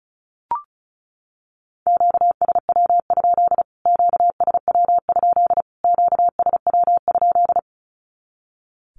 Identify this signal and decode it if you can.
Morse code, heard as a keyed tone, about 34 wpm, 700 Hz. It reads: QSW? QSW? QSW?
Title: Q-Codes Repeated 3x 35wpm